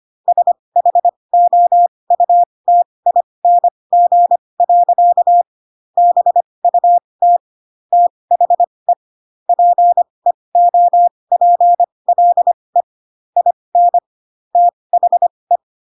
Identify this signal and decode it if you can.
Morse code, heard as a keyed tone, about 25 wpm, spaced as for 20 wpm, 700 Hz. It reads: SHOUTING. BUT THE PEOPLE IN THE